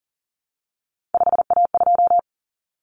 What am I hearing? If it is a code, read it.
Morse code, heard as a keyed tone, about 40 wpm, 700 Hz. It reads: 5A2
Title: Common Cut Numbers - 3-Digit 40wpm